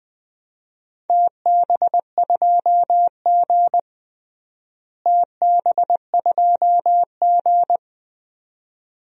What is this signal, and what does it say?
Morse code, heard as a keyed tone, about 20 wpm, 700 Hz. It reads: TB2G TB2G
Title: DXCC-100 Call Signs - 4 Characters - Repeated 2x - 20wpm